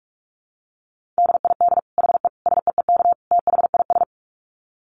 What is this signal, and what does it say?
Morse code, heard as a keyed tone, about 44 wpm, 700 Hz. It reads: DID HE SEEK THIS
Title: Dit Challenge 45wpm